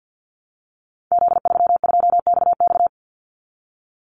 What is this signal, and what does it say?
Morse code, heard as a keyed tone, about 50 wpm, 700 Hz. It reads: Z32XX
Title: Call Signs Repeated 2x - 50wpm